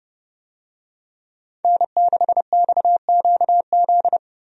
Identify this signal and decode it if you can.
Morse code, heard as a keyed tone, about 30 wpm, 700 Hz. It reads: N6XQZ
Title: USA Call Signs - 5 Characters - Repeated 2x - 30wpm